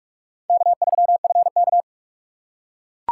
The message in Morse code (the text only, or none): K3FK